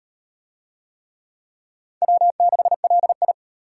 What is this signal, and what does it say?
Morse code, heard as a keyed tone, about 38 wpm, 700 Hz. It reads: W6LI